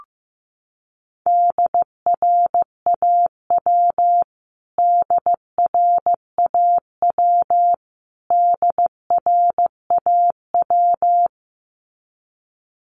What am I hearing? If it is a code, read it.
Morse code, heard as a keyed tone, about 15 wpm, 700 Hz. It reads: DRAW DRAW DRAW
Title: Top 300 Words Repeated 15wpm